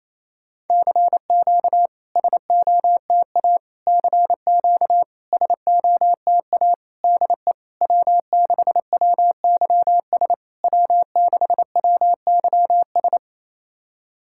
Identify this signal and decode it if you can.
Morse code, heard as a keyed tone, about 28 wpm, 700 Hz. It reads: CQ SOTA CQ SOTA DE W6WYS W6WYS
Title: SOTA QSO Practice 28wpm